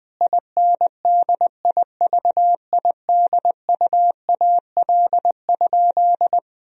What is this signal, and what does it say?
Morse code, heard as a keyed tone, about 20 wpm, 700 Hz. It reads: INDIVIDUAL?
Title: Sentences from Top 1000 Words - Review Part 4 - 20wpm